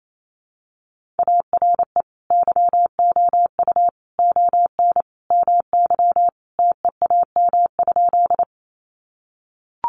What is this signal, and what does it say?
Morse code, heard as a keyed tone, about 28 wpm, 700 Hz. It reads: ARE YOU ON MY TEAM?
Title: Sentences from Top 1500 Words 28wpm